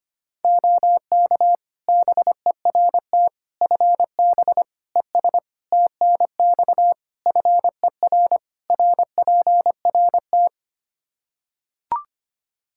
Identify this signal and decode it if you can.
Morse code, heard as a keyed tone, about 25 wpm, 700 Hz. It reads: OK BERT FB ES TNX FER RPRT
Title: Practice QSO - Exchange 2 - Info - 25wpm